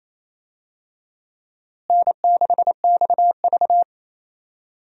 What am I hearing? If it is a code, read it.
Morse code, heard as a keyed tone, about 28 wpm, 700 Hz. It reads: N6XV
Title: USA Call Signs - 4 Characters - Repeated 3x - 28wpm